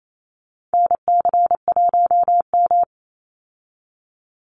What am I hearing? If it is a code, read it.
Morse code, heard as a keyed tone, about 28 wpm, 700 Hz. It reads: NC1M
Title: Call Signs Repeated 2x - 28wpm